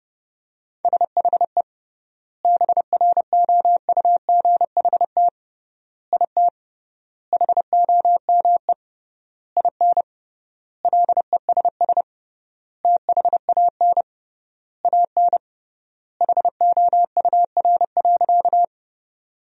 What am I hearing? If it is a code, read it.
Morse code, heard as a keyed tone, about 30 wpm, 700 Hz. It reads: SHE BROUGHT IT HOME IN LESS THAN AN HOUR.
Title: 3x Word-Spacing - Sentences from Top 500 Words - 30wpm